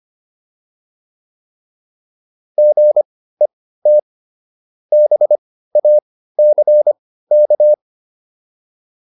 Morse code, heard as a keyed tone, about 25 wpm, spaced as for 15 wpm, 600 Hz. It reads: GET BACK
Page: intermediate-practice-files